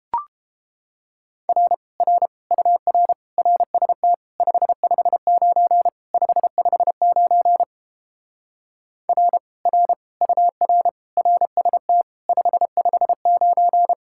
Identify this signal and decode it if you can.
Morse code, heard as a keyed tone, about 31 wpm, 700 Hz. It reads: R R UR RST 559 559 R R UR RST 559